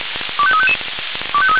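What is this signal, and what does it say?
Five-tone selective call (ZVEI), heard as quick runs of five tones, tones of about 70 ms. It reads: 25355
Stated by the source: Five-tone ZVEI selective call received by SDR.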